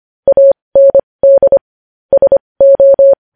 Morse code, heard as a keyed tone, about 25 wpm, spaced as for 20 wpm, 550 Hz. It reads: AND SO
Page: news-headlines-in-morse-code-at-20-wpm